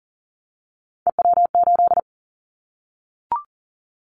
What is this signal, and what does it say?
Morse code, heard as a keyed tone, about 40 wpm, 700 Hz. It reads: EW8
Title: Callsign Prefixes for Top 50 DXCC - 40wpm